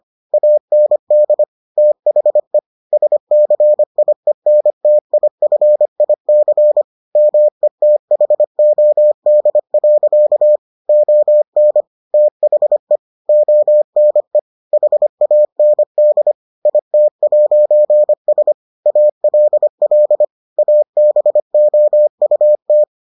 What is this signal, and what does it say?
Morse code, heard as a keyed tone, about 25 wpm, 600 Hz. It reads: AND THE SCIENTIFIC METHOD. ON THE ONE HAND IT'S ALL ABOUT